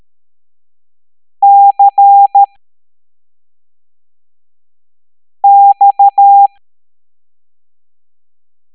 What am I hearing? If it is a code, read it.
Morse code, heard as a keyed tone, about 13 wpm, 800 Hz. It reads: C X
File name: hamdata / L03-EX01-IGC